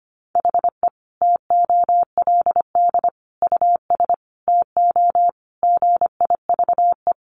Morse code, heard as a keyed tone, about 25 wpm, 700 Hz. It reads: HE TOLD US TO GIVE
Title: Sentences from Top 1000 Words 25wpm